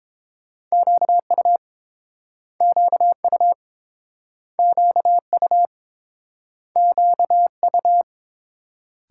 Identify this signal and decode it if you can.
Morse code, heard as a keyed tone, about 30 wpm, 700 Hz. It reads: QU QU QU QU